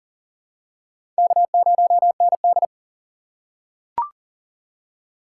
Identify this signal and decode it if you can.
Morse code, heard as a keyed tone, about 40 wpm, 700 Hz. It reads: K0ND
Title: USA Call Signs - 4x Character Spacing - 40wpm